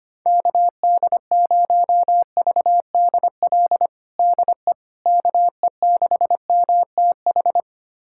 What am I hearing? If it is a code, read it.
Morse code, heard as a keyed tone, about 25 wpm, 700 Hz. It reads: KD0VDL DE KE6MTH
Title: USA Callsign DE Callsign - 6 Characters - 25wpm